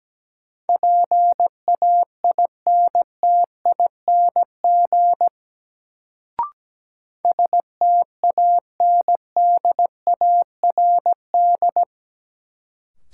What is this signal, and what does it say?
Morse code, heard as a keyed tone, about 17 wpm, 700 Hz. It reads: PAINTING STANDARD
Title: Sentences from Top 1000 Words - Review Part 6 - New Word Drill - 17wpm